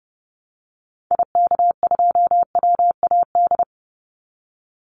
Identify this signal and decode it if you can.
Morse code, heard as a keyed tone, about 30 wpm, 700 Hz. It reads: IK2WAD
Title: Call Signs Repeated 2x - 30wpm